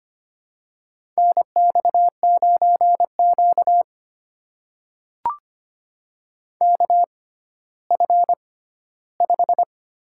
Morse code, heard as a keyed tone, about 25 wpm, spaced as for 14 wpm, 700 Hz. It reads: NX9Q KF5
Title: USA Call Signs - 6x Character Spacing - 25wpm